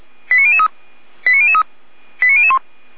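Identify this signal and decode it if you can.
Five-tone selective call (ZVEI), heard as quick runs of five tones, tones of about 70 ms. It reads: 89003 89003 89001
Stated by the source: ZVEI five-tone call 89160 received by SDR, from audio.f85360.feuerwehr-orlamuende.org, fire brigade Orlamünde.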